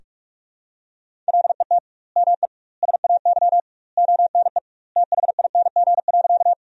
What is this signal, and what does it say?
Morse code, heard as a keyed tone, about 46 wpm, 700 Hz. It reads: LET ME SAY ONE THING.
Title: 2x Word-Spacing - Sentences from Top 300 Words - 45wpm